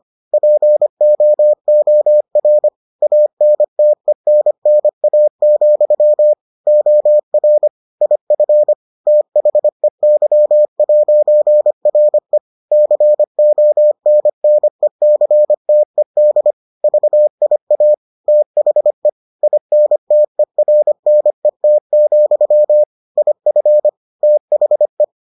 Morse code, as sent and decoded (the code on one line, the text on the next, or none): .--. --- --- .-. / .- -. - . -. -. .- --..-- / --- .-. / .. ..-. / - .... -.-- .----. .-. / -.-. --- -. -. -.-. - . -.. / ...- .. .- / - .... / .. -. - . .-. -. - --..-- / .. ..-. / - .... .
POOR ANTENNA, OR IF THEY'RE CONNECTED VIA THE INTERNET, IF THE